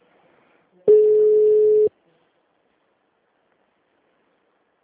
Ringback tone (Europe, 425 Hz).
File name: out- -2022- - - .13543